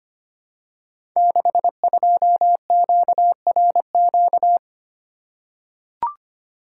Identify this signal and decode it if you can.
Morse code, heard as a keyed tone, about 25 wpm, 700 Hz. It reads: 62QRQ
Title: Code Groups - 5 Letter:Numbers - 25wpm